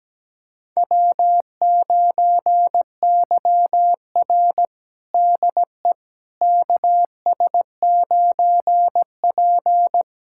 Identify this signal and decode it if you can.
Morse code, heard as a keyed tone, about 17 wpm, 700 Hz. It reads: W9YR DE KS9P
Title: USA Callsign DE Callsign - 4 Characters - 17wpm